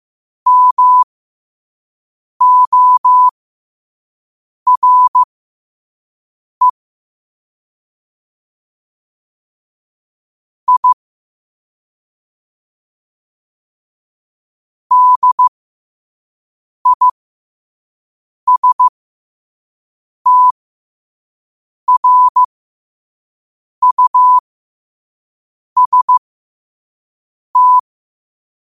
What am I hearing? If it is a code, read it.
Morse code, heard as a keyed tone, about 15 wpm, spaced as for 5 wpm, 1000 Hz. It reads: MORE I DISTRUST
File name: 5wpm